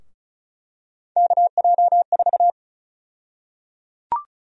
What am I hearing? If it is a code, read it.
Morse code, heard as a keyed tone, about 36 wpm, 700 Hz. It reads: KJ4
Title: USA Call Sign Prefixes - Repeated 2x - 35wpm